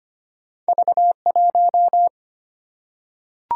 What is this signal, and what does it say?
Morse code, heard as a keyed tone, about 25 wpm, 700 Hz. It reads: V1
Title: Single-Letter Single-Number 25wpm